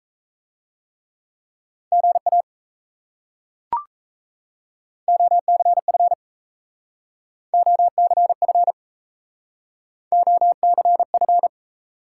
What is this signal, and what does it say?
Morse code, heard as a keyed tone, about 34 wpm, 700 Hz. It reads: GA OCF OCF OCF
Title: Speed-Racing Top 180 QSO Elements 28wpm